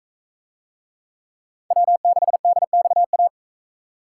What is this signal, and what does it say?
Morse code, heard as a keyed tone, about 43 wpm, 700 Hz. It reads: W6DXA